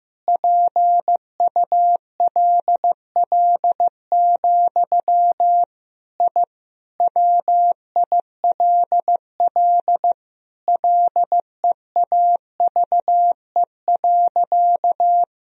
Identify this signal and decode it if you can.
Morse code, heard as a keyed tone, about 15 wpm, 700 Hz. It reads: PULL, I WILL LEAVE.
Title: Sentences from Top 1000 Words - Review Part 2 - 15wpm